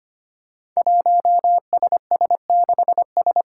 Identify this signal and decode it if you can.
Morse code, heard as a keyed tone, about 25 wpm, 700 Hz. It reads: 1SS6S
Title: Five Challenging Letter-Number Combinations 25wpm